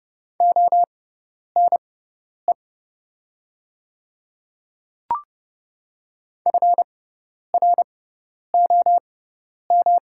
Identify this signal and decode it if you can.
Morse code, heard as a keyed tone, about 31 wpm, spaced as for 10 wpm, 700 Hz. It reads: ONE FROM
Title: 6x Character Spacing - Top 100 Words - 30wpm